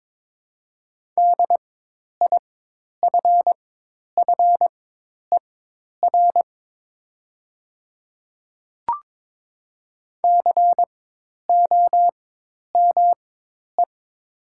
Morse code, heard as a keyed tone, about 22 wpm, spaced as for 10 wpm, 700 Hz. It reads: DIFFER COME
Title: 4x Character Spacing - Top 200 Words - 22wpm